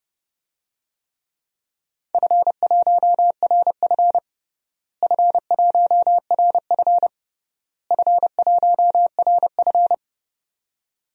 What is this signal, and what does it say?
Morse code, heard as a keyed tone, about 30 wpm, 700 Hz. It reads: F1RF F1RF F1RF
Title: DXCC-100 Call Signs - 4 Characters - Repeated 3x - 30wpm